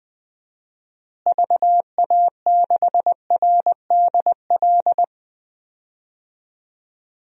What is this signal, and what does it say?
Morse code, heard as a keyed tone, about 20 wpm, 700 Hz. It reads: VA6RDL
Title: DXCC-100 Call Signs - 6 Characters - Repeated 3x - 20wpm